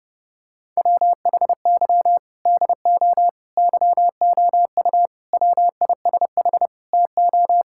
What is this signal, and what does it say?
Morse code, heard as a keyed tone, about 30 wpm, 700 Hz. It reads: WHY DO YOU WISH TO
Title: Sentences from Top 1600 Words 30wpm